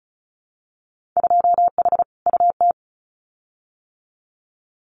Morse code, heard as a keyed tone, about 36 wpm, 700 Hz. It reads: 2H UT
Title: Winter Field Day Exchanges Encore 35wpm